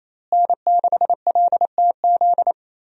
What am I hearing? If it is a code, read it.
Morse code, heard as a keyed tone, about 28 wpm, 700 Hz. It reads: N6LTZ
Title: USA Call Signs - 5 Characters - Repeated 3x - 28wpm